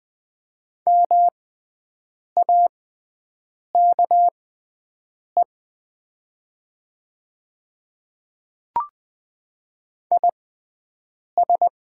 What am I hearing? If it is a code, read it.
Morse code, heard as a keyed tone, about 20 wpm, spaced as for 7 wpm, 700 Hz. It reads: MAKE IS
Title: 6x Character Spacing - Top 300 Words - 20wpm